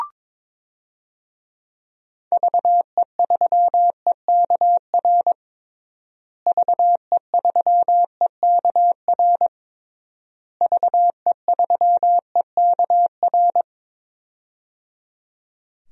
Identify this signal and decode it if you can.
Morse code, heard as a keyed tone, about 22 wpm, 700 Hz. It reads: VE3EKR VE3EKR VE3EKR
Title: DXCC-100 Call Signs - 6 Characters - Repeated 3x - 22wpm